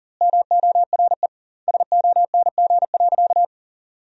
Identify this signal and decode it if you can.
Morse code, heard as a keyed tone, about 40 wpm, 700 Hz. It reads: MORE SONG.
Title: 2x Word-Spacing - Sentences from Top 400 Words - 40wpm